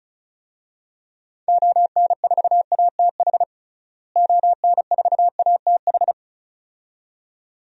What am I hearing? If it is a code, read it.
Morse code, heard as a keyed tone, about 36 wpm, 700 Hz. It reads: ON4ATH ON4ATH
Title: DXCC-100 Call Signs - 6 Characters - Repeated 2x - 35wpm